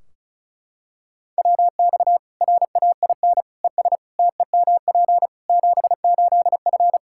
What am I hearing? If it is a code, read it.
Morse code, heard as a keyed tone, about 36 wpm, 700 Hz. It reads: WX RAIN ES TEMP 78F
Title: K1USN SST QSO Practice 35wpm